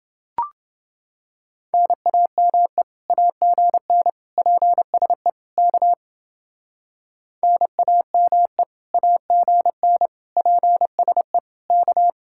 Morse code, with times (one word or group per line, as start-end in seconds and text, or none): 1.7-2.8 NAME
3.1-4.1 AGN
4.4-5.3 PSE
5.6-5.9 K
7.4-8.6 NAME
8.9-10.1 AGN
10.4-11.4 PSE
11.7-12.1 K